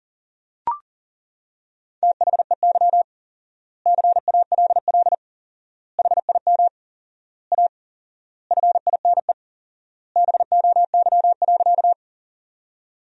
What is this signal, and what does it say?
Morse code, heard as a keyed tone, about 40 wpm, 700 Hz. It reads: THEY CALL HIM A FINE BOY.
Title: 4x Word-Spacing - Sentences from Top 500 Words - 40wpm